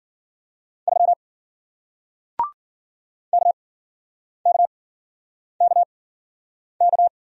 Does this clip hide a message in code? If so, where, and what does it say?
Yes: Morse code from 0.9 to 7.1 s, heard as a keyed tone, about 53 wpm, spaced as for 9 wpm, 700 Hz. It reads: 3 KKKK